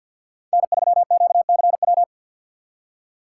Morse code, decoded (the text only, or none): N3QCW